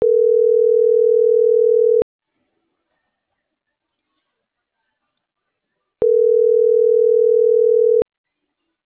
Ringback tone (North America), 440 + 480 Hz.